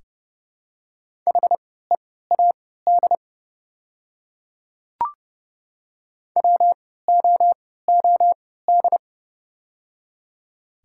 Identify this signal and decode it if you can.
Morse code, heard as a keyed tone, about 29 wpm, spaced as for 17 wpm, 700 Hz. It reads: HEAD WOOD
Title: 3x Character Spacing - Top 300 Words - 30wpm